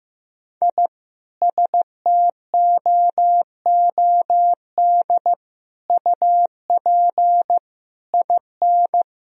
Morse code, heard as a keyed tone, about 16 wpm, 700 Hz. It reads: I STOOD UP IN
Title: Sentences from Top 1000 Words - Review Part 2 - 15wpm